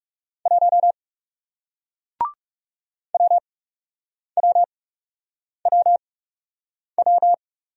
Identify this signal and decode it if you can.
Morse code, heard as a keyed tone, about 40 wpm, spaced as for 8 wpm, 700 Hz. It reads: 1 WWWW